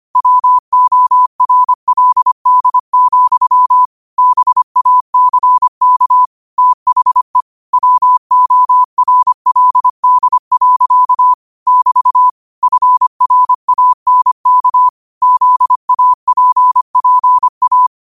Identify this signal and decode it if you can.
Morse code, heard as a keyed tone, about 25 wpm, 1000 Hz. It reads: WORLD, BACK THE WORLD. <BT> FRANK ZAPPA
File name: qotdfiles / 25wpm